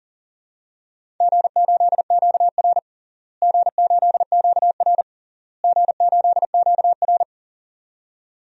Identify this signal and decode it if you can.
Morse code, heard as a keyed tone, about 40 wpm, 700 Hz. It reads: G8QR G8QR G8QR
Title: DXCC-100 Call Signs - 4 Characters - Repeated 3x - 40wpm